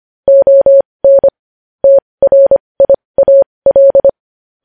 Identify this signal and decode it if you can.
Morse code, heard as a keyed tone, about 25 wpm, spaced as for 20 wpm, 550 Hz. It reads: ON TRIAL
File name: podcast / News headlines-1x-25x20